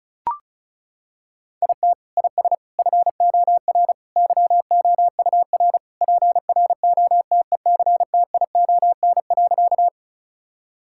Morse code, heard as a keyed tone, about 34 wpm, 700 Hz. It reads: IT IS FOR YOUR PROTECTION.
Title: Sentences from Top 1600 Words 35wpm